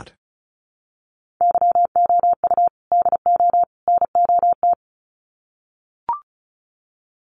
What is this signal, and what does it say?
Morse code, heard as a keyed tone, about 34 wpm, 700 Hz. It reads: YOU DO NOT
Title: Sets of 3 Words 35wpm